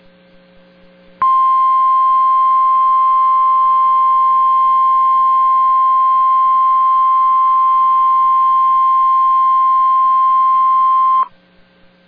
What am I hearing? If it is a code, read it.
Emergency Alert System (SAME) tones (the 1050 Hz attention tone). It is heard from 1.2 to 11.3 s.